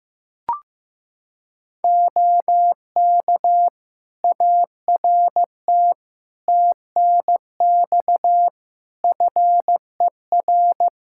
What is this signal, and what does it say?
Morse code, heard as a keyed tone, about 15 wpm, 700 Hz. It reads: OK ART TNX FER
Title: Practice QSO - Exchange 4 - Ending - 15wpm